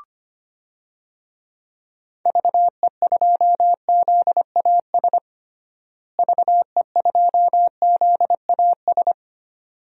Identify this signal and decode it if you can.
Morse code, heard as a keyed tone, about 25 wpm, 700 Hz. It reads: VE2ZAS VE2ZAS
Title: DXCC-100 Call Signs - 6 Characters - Repeated 2x - 25wpm